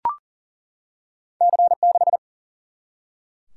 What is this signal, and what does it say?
Morse code, heard as a keyed tone, about 39 wpm, 700 Hz. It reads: C 6
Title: Single-Letter Single-Number 40wpm